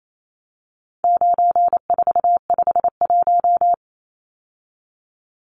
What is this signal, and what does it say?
Morse code, heard as a keyed tone, about 28 wpm, 700 Hz. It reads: 9451
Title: Numbers - 4-Digit 28wpm